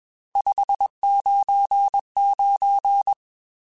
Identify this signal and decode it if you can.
Morse code, heard as a keyed tone, about 21 wpm, 800 Hz. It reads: 599
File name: RSTs / RST-mp3s / Various RST @20